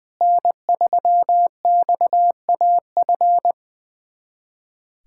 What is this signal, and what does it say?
Morse code, heard as a keyed tone, about 20 wpm, 700 Hz. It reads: N3XAF
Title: USA Callsign DE Callsign 20wpm